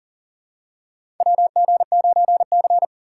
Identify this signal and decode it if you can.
Morse code, heard as a keyed tone, about 39 wpm, 700 Hz. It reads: WG9C
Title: USA Call Signs - 4 Characters - 4x Character Spacing - 40wpm